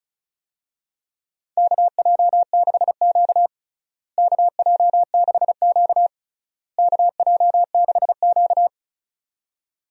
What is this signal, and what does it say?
Morse code, heard as a keyed tone, about 36 wpm, 700 Hz. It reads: KJ6Q KJ6Q KJ6Q
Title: USA Call Signs - 4 Characters - Repeated 3x - 35wpm